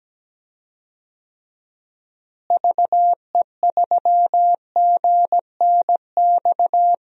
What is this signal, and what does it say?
Morse code, heard as a keyed tone, about 17 wpm, 700 Hz. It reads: VE3GNX
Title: DXCC-100 Call Signs - 6 Characters - Repeated 2x - 17wpm